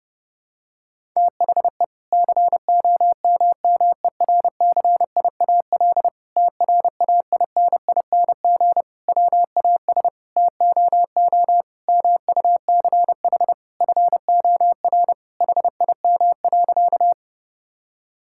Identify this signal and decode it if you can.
Morse code, heard as a keyed tone, about 32 wpm, 700 Hz. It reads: THE COMMERCIAL TRAINING WAS TOO MUCH FOR HIM.
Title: Sentences from Top 1000 Words - Review Part 5 - 30wpm